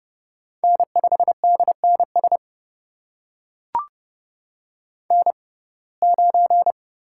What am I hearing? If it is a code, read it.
Morse code, heard as a keyed tone, about 30 wpm, spaced as for 22 wpm, 700 Hz. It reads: N5DNS N9
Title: USA Call Signs - 5 Characters - 6x Character Spacing - 30wpm